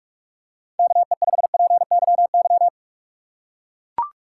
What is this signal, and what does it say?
Morse code, heard as a keyed tone, about 44 wpm, 700 Hz. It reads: KE5PYY